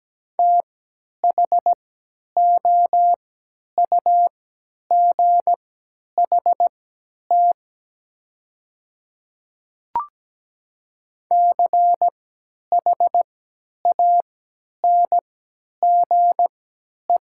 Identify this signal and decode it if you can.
Morse code, heard as a keyed tone, about 17 wpm, spaced as for 10 wpm, 700 Hz. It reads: THOUGHT CHANGE
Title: 3x Character Spacing - Top 300 Words - 17wpm